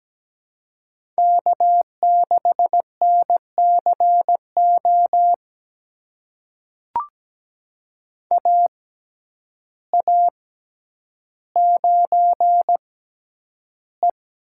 Morse code, heard as a keyed tone, about 17 wpm, spaced as for 9 wpm, 700 Hz. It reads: K6NCO AA9E
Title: USA Call Signs - 5 Characters - 6x Character Spacing - 17wpm